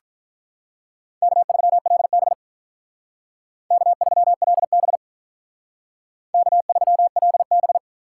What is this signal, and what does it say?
Morse code, heard as a keyed tone, about 45 wpm, 700 Hz. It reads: K3LB K3LB K3LB